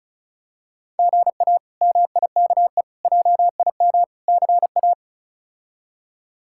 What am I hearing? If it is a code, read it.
Morse code, heard as a keyed tone, about 34 wpm, 700 Hz. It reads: GA MIKE JIM CA
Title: K1USN SST QSO Practice - 35wpm